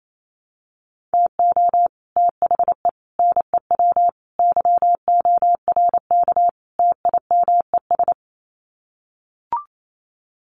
Morse code, heard as a keyed tone, about 28 wpm, 700 Hz. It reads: TO THE NEW YORK TIMES
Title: Sets of 5 Words Encore2 28wpm